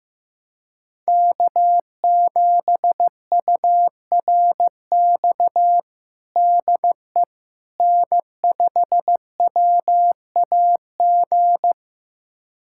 Morse code, heard as a keyed tone, about 15 wpm, 700 Hz. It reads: K7URX DE N5WAG
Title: USA Callsign DE Callsign - 5 Characters - 15wpm